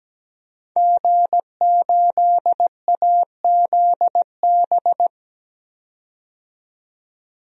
Morse code, heard as a keyed tone, about 17 wpm, 700 Hz. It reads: G8AZB
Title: DXCC-100 Call Signs - 5 Characters - 17wpm